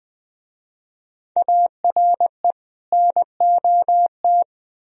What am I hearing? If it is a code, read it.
Morse code, heard as a keyed tone, about 23 wpm, 700 Hz. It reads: ARE NOT
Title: Speed-Racing - Sets of 2 Words 20wpm